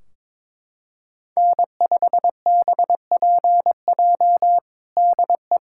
Morse code, heard as a keyed tone, about 22 wpm, 700 Hz. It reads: N5BPJ DE
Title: K1USN SST QSO Practice 22wpm